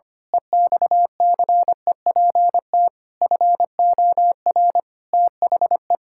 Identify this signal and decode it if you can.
Morse code, heard as a keyed tone, about 25 wpm, 700 Hz. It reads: EXCEPT FOR THE
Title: Pangram Challenge v2 - 25wpm